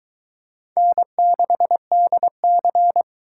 Morse code, heard as a keyed tone, about 23 wpm, 700 Hz. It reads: N6DC